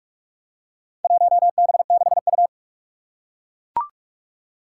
Morse code, heard as a keyed tone, about 44 wpm, 700 Hz. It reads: 1B6U
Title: Four Challenging Letter-Number Combinations 45wpm